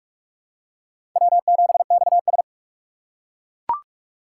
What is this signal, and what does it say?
Morse code, heard as a keyed tone, about 42 wpm, 700 Hz. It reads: W7XS
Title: USA Call Signs - 4 Characters - 6x Character Spacing - 45wpm